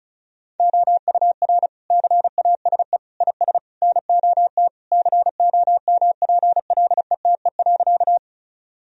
Morse code, heard as a keyed tone, about 34 wpm, 700 Hz. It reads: OUR CASE IS NOT COMPLETE.
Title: Sentences from Top 1000 Words - Review Part 2 - 35wpm